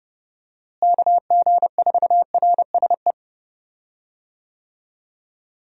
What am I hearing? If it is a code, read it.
Morse code, heard as a keyed tone, about 30 wpm, 700 Hz. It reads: KG4RSE
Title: USA Call Signs - 6 Characters - Repeated 2x - 30wpm